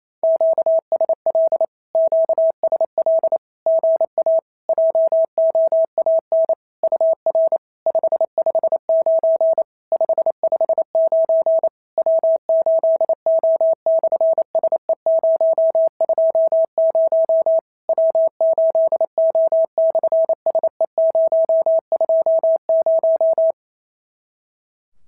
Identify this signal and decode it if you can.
Morse code, heard as a keyed tone, about 28 wpm, 650 Hz. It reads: QSL QSL GA JOAN UR 559 559 W8O/SE020 W8O/SE020
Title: SOTA QSO Practice 28wpm